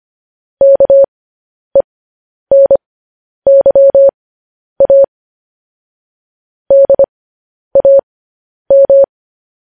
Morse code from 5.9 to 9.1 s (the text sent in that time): DAM